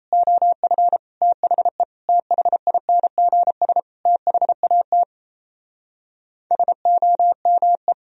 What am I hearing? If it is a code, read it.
Morse code, heard as a keyed tone, about 37 wpm, 700 Hz. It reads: OF THE THINGS THAT SOME